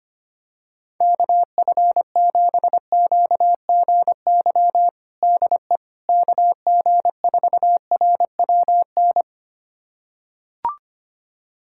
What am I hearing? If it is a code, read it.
Morse code, heard as a keyed tone, about 25 wpm, 700 Hz. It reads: KF7QGY DE KG4RWN
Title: USA Callsign DE Callsign - 6 Characters - 25wpm